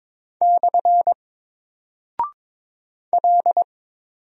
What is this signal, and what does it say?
Morse code, heard as a keyed tone, about 22 wpm, 700 Hz. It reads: / L
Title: Single Character (Letter:Number:Punctuation:Prosign) 22wpm